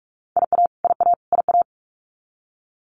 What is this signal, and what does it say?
Morse code, heard as a keyed tone, about 44 wpm, 700 Hz. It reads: IA IA IA
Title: US States Repeated 3x 45wpm